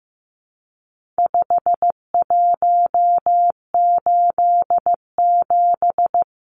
Morse code, heard as a keyed tone, about 15 wpm, 700 Hz. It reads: 5187
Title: Numbers - 4-Digit 15wpm